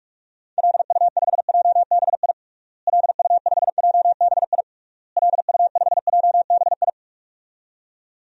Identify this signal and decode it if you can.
Morse code, heard as a keyed tone, about 46 wpm, 700 Hz. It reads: LU5JBI LU5JBI LU5JBI
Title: DXCC-100 Call Signs - 6 Characters - Repeated 3x - 45wpm